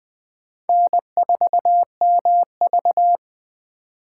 Morse code, heard as a keyed tone, about 20 wpm, 700 Hz. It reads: N4MV
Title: Speed Racing - USA Call Signs 17wpm